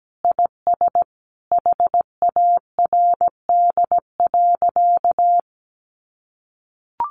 Morse code, heard as a keyed tone, about 17 wpm, 700 Hz. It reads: IS HARD.
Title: Sentences from Top 600 Words 17wpm